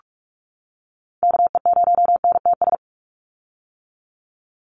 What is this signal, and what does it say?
Morse code, heard as a keyed tone, about 44 wpm, 700 Hz. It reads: KE0NTS
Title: USA Call Signs 45wpm